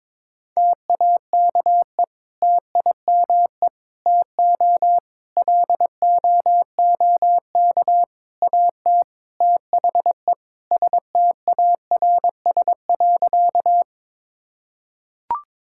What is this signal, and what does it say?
Morse code, heard as a keyed tone, about 22 wpm, 700 Hz. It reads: TAKE TIME TO LOOK AT THE STARS.
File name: practice / 22wpm / Sentences with 3-Word Phrases - 22wpm